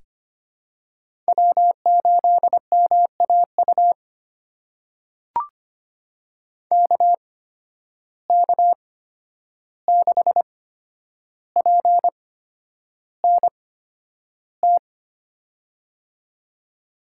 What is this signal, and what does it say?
Morse code, heard as a keyed tone, about 25 wpm, spaced as for 10 wpm, 700 Hz. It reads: W8MAU KK6PNT
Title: USA Call Signs - 8x Character Spacing - 25wpm